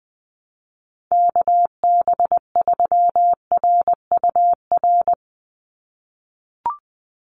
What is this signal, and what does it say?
Morse code, heard as a keyed tone, about 20 wpm, 700 Hz. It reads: KB3RUR
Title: USA Call Signs 20wpm